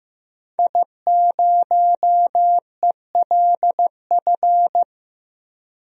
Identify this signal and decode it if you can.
Morse code, heard as a keyed tone, about 15 wpm, 700 Hz. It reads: I0ELF
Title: DXCC-100 Call Signs - 5 Characters - Repeated 2x - 15wpm